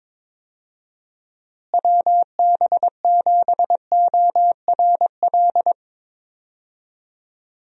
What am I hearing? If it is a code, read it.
Morse code, heard as a keyed tone, about 22 wpm, 700 Hz. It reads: WB7ORL
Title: USA Call Signs - 6 Characters - Repeated 3x - 22wpm